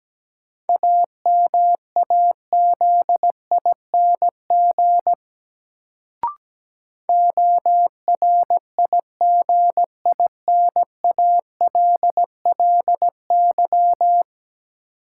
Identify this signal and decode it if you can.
Morse code, heard as a keyed tone, about 17 wpm, 700 Hz. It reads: AMAZING ORIGINALLY
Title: Sentences from Top 1500 Words - New Word Drill Alt - 17wpm